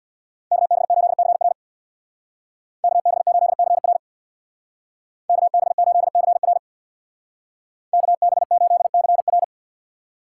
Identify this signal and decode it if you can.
Morse code, heard as a keyed tone, about 58 wpm, 700 Hz. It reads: KB8CR KB8CR KB8CR KB8CR
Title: Speed Racing - USA 2x2 Call Signs 50wpm